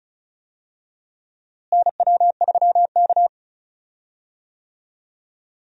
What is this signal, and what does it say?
Morse code, heard as a keyed tone, about 36 wpm, 700 Hz. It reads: NW3K
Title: USA Call Signs - 4 Characters - Repeated 2x - 35wpm